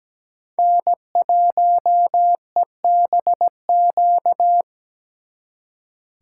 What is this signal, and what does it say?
Morse code, heard as a keyed tone, about 20 wpm, 700 Hz. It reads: N1EBQ
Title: Speed Racing - USA 1x3 Call Signs 17wpm